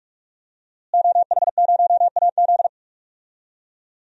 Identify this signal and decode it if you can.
Morse code, heard as a keyed tone, about 42 wpm, 700 Hz. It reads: OH0AZ
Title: DXCC-100 Call Signs - 5 Characters - Repeated 3x - 45wpm